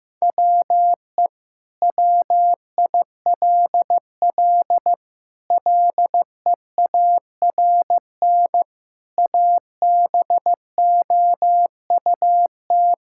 Morse code, heard as a keyed tone, about 15 wpm, 700 Hz. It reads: WE WILL LEARN ABOUT
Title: Sentences from Top 1000 Words - Review Part 5 - 15wpm